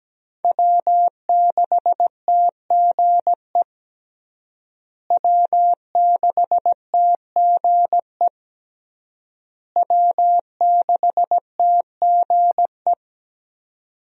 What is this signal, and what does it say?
Morse code, heard as a keyed tone, about 17 wpm, 700 Hz. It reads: W6TGE W6TGE W6TGE
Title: USA Call Signs - 5 Characters - Repeated 3x - 17wpm